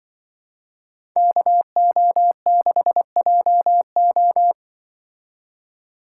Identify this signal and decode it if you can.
Morse code, heard as a keyed tone, about 23 wpm, 700 Hz. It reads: KO6JO
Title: Speed Racing - USA 2x2 Call Signs 20wpm